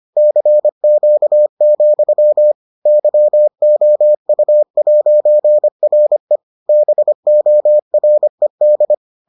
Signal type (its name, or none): Morse code